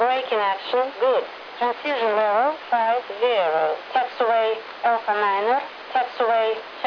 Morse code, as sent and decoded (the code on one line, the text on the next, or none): none